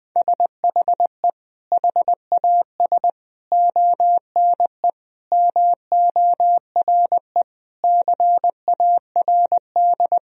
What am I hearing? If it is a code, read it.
Morse code, heard as a keyed tone, about 20 wpm, 700 Hz. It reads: SHE HAS ONE MORE CARD
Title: Sentences from Top 1000 Words - Review Part 4 - 20wpm